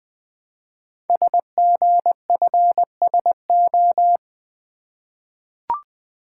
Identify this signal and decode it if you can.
Morse code, heard as a keyed tone, about 20 wpm, 700 Hz. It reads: SGFSO
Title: Code Groups - 5 Letters - 20wpm